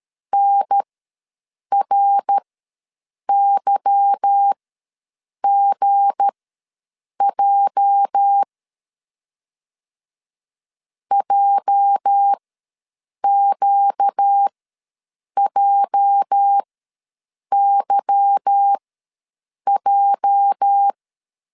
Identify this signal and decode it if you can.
Morse code, heard as a keyed tone, about 13 wpm, spaced as for 7 wpm, 800 Hz. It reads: NRYGJ JQJYJ
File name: LEZIONI / 8 YJ45 / 8 YJ ws 12wpm 3342